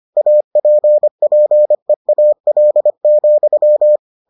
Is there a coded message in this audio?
Yes: Morse code from 0.2 to 4.0 s, heard as a keyed tone, about 25 wpm, 600 Hz. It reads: APPEAL,